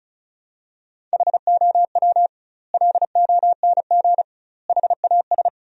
Morse code, heard as a keyed tone, about 36 wpm, 700 Hz. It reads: HOW LONG HAS
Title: 2x Word-Spacing - Sentences from Top 100 Words - 35wpm